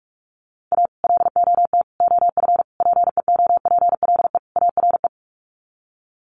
Morse code, heard as a keyed tone, about 44 wpm, 700 Hz. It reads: A LOT OF PEOPLE ARE
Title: Sets of 5 Words 45wpm